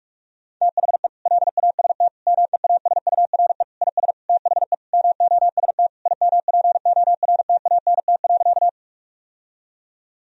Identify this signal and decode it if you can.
Morse code, heard as a keyed tone, about 43 wpm, 700 Hz. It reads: THE LAST MEASURE IS THE MOST IMPORTANT.
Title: Sentences from Top 1000 Words - Review Part 2 - 45wpm